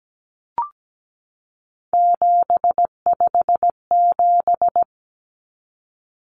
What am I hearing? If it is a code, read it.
Morse code, heard as a keyed tone, about 17 wpm, 700 Hz. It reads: 757
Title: Common Cut Numbers - 3-Digit 17wpm